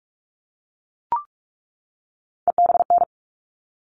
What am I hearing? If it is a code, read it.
Morse code, heard as a keyed tone, about 44 wpm, 700 Hz. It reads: EBN
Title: All Cut Numbers - 3-Digit 45wpm